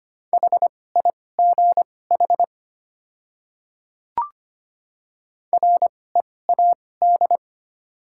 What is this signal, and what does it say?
Morse code, heard as a keyed tone, about 25 wpm, spaced as for 18 wpm, 700 Hz. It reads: HIGH READ
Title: 2x Character Spacing - Top 300 Words - 25wpm